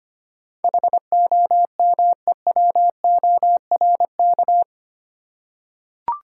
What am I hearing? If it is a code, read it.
Morse code, heard as a keyed tone, about 25 wpm, 700 Hz. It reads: HOMEWORK
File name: Eight Letter Words - 25wpm